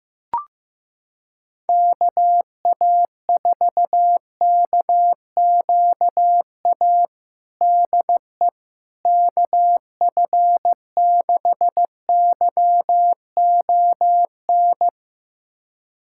Morse code, heard as a keyed tone, about 15 wpm, 700 Hz. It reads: KA4KQA DE KF6YON
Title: USA Callsign DE Callsign 15wpm